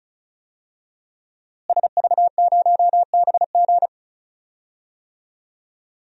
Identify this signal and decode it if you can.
Morse code, heard as a keyed tone, about 35 wpm, 700 Hz. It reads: SV0BG
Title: DXCC-100 Call Signs - 5 Characters - Repeated 3x - 35wpm